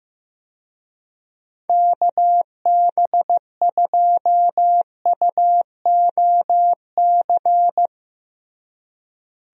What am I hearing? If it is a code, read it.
Morse code, heard as a keyed tone, about 16 wpm, 700 Hz. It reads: KB2UOC
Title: USA Call Signs - 6 Characters - Repeated 2x - 15wpm